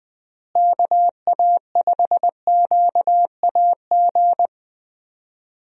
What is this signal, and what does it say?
Morse code, heard as a keyed tone, about 20 wpm, 700 Hz. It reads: KA5QAG
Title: USA Call Signs - 6 Characters - Repeated 3x - 20wpm